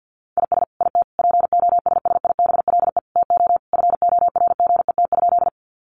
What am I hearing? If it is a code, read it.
Morse code, heard as a keyed tone, about 48 wpm, 700 Hz. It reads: IS IT POSSIBLE TO FORGET?
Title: Sentences from Top 1200 Words 50wpm